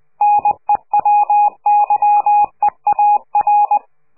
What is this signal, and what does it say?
Morse code, heard as a keyed tone, about 20 wpm, 830 Hz. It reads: NEW YEAR
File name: CQ de PCH laatste bericht farewell